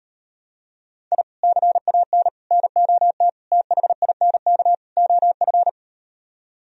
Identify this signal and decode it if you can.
Morse code, heard as a keyed tone, about 38 wpm, 700 Hz. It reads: I CAN NOT THINK OF